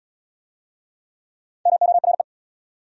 Morse code, heard as a keyed tone, about 77 wpm, 700 Hz. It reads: MORE